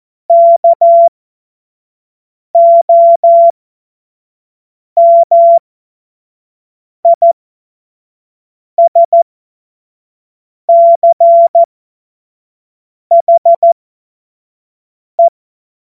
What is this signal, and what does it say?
Morse code, heard as a keyed tone, about 14 wpm, spaced as for 5 wpm, 680 Hz. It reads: KOMISCHE
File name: Morsekurs / Witze / Witz 21. Februar 2023 25 F70